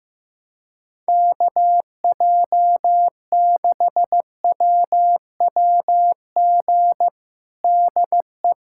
Morse code, heard as a keyed tone, about 15 wpm, 700 Hz. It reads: KJ6WWG DE
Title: USA Callsign DE Callsign - 6 Characters - 15wpm